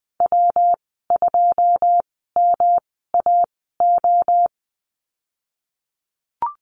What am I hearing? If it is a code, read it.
Morse code, heard as a keyed tone, about 20 wpm, spaced as for 15 wpm, 700 Hz. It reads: W2MAO
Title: 173 Lesson 28g - Number 9 - Call Signs 20wpm